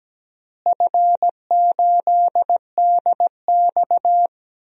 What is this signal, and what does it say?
Morse code, heard as a keyed tone, about 17 wpm, 700 Hz. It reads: F8DX
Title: DXCC-100 Call Signs - 4 Characters - Repeated 2x - 17wpm